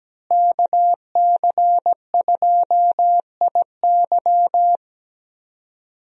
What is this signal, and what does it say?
Morse code, heard as a keyed tone, about 17 wpm, 700 Hz. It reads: KC2IY
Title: USA Call Signs - 5 Characters - Repeated 3x - 17wpm